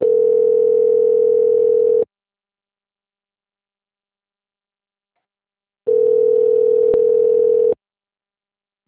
Ringback tone (North America), 440 + 480 Hz.